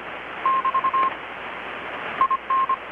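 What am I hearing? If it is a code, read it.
Morse code, heard as a keyed tone, about 25 wpm, 1060 Hz. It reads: <BT> IN
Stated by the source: Morse from KPH's Night of Nights XX maritime radiotelegraphy event, received in Chiapas, Mexico about 3,800 km away on a Kenwood R-600 with a grounded 10 m long-wire antenna.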